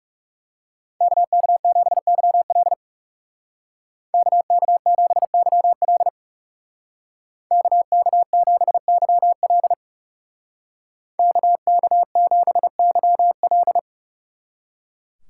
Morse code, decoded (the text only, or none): KK7YL KK7YL KK7YL KK7YL